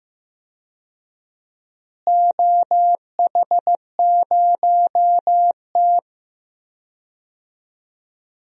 Morse code, heard as a keyed tone, about 15 wpm, 700 Hz. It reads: OH0T
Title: DXCC-100 Call Signs - 4 Characters - Repeated 2x - 15wpm